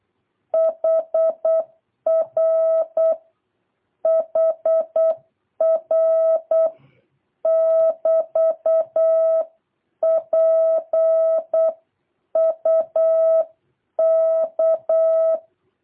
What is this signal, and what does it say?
Morse code, heard as a keyed tone, about 8 wpm, 650 Hz. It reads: HR HR <BT> PUK